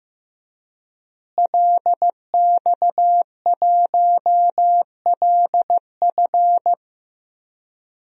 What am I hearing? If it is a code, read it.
Morse code, heard as a keyed tone, about 16 wpm, 700 Hz. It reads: LX1LF
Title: DXCC-100 Call Signs - 5 Characters - Repeated 3x - 15wpm